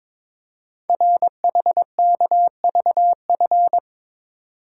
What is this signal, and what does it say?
Morse code, heard as a keyed tone, about 22 wpm, 700 Hz. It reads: RHKVF
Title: Code Groups - 5 Letters - 22wpm